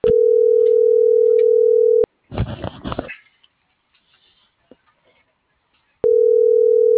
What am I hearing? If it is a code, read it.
Ringback tone (North America), 440 + 480 Hz.